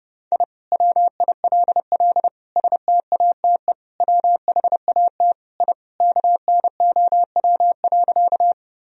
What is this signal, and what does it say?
Morse code, heard as a keyed tone, about 32 wpm, 700 Hz. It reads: I WILL STATE WHAT I KNOW.
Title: Sentences from Top 1000 Words - Review Part 2 - 30wpm